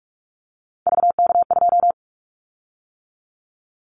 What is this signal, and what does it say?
Morse code, heard as a keyed tone, about 46 wpm, 690 Hz. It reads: VK2
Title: Callsign Prefixes for Top 50 DXCC - 45wpm